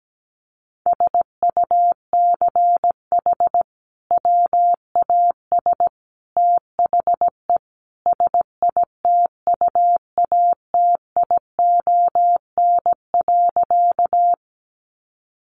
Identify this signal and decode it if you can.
Morse code, heard as a keyed tone, about 17 wpm, 700 Hz. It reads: SUCH WAS THE SITUATION.
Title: Sentences from Top 1200 Words 17wpm